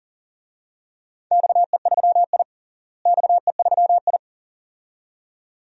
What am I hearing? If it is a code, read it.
Morse code, heard as a keyed tone, about 40 wpm, 700 Hz. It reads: XE3I XE3I
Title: DXCC-100 Call Signs - 4 Characters - Repeated 2x - 40wpm